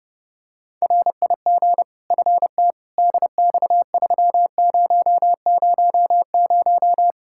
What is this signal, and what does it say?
Morse code, heard as a keyed tone, about 32 wpm, 700 Hz. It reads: RIG FT DX3000
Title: Practice QSO - Exchange 2 - Info - 30wpm